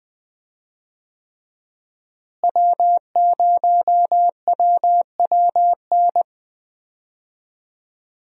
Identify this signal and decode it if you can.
Morse code, heard as a keyed tone, about 20 wpm, 700 Hz. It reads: W0WWN
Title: USA Call Signs - 5 Characters - Repeated 3x - 20wpm